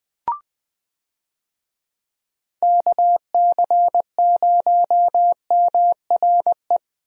Morse code, heard as a keyed tone, about 21 wpm, 700 Hz. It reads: KC0MRE
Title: USA Call Signs - 6 Characters - Repeated 2x - 20wpm